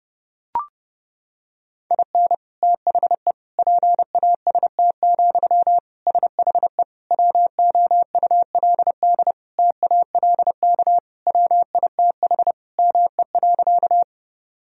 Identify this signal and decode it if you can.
Morse code, heard as a keyed tone, about 30 wpm, 700 Hz. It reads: IN THE PAST, SHE WOULD TALK WITH ME.
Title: Sentences from Top 1000 Words - Review Part 3 - 30wpm